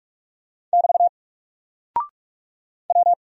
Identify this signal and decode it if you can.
Morse code, heard as a keyed tone, about 44 wpm, 700 Hz. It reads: <BT> W
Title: Single Character (Letter:Number:Punctuation:Prosign) Alt 45wpm